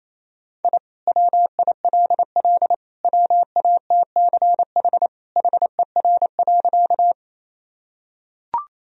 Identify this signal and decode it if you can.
Morse code, heard as a keyed tone, about 28 wpm, 700 Hz. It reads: I WILL WATCH HER.
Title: Sentences from Top 1000 Words - Review Part 1 - 28wpm